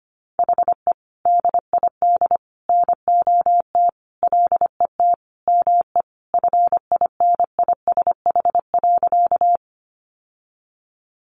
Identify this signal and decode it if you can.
Morse code, heard as a keyed tone, about 25 wpm, 700 Hz. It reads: HE DID NOT LET ME FINISH.
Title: Sentences from Top 1200 Words 25wpm